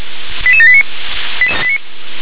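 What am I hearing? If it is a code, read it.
Five-tone selective call (ZVEI), heard as quick runs of five tones, tones of about 70 ms. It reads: 88789
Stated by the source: Signal received via SDR, ZVEI five-tone.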